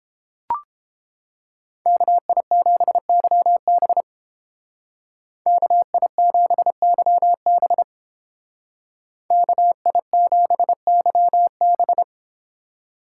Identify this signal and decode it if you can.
Morse code, heard as a keyed tone, about 30 wpm, 700 Hz. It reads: KI7YB KI7YB KI7YB